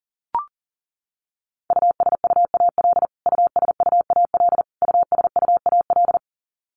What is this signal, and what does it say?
Morse code, heard as a keyed tone, about 40 wpm, 700 Hz. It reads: USUAL USUAL USUAL
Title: Top 500 Words Repeated 40wpm